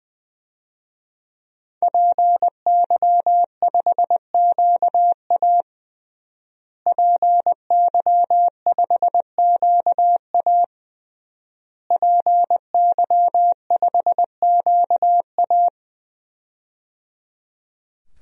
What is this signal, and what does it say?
Morse code, heard as a keyed tone, about 21 wpm, 700 Hz. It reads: PY5QA PY5QA PY5QA
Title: DXCC-100 Call Signs - 5 Characters - Repeated 3x - 20wpm